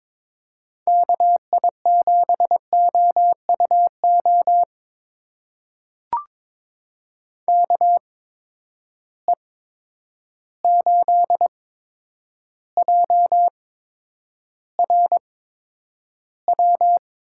Morse code, heard as a keyed tone, about 23 wpm, spaced as for 10 wpm, 700 Hz. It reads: KI7OUO KE8JRW
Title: USA Call Signs - 6 Characters - 8x Character Spacing - 22wpm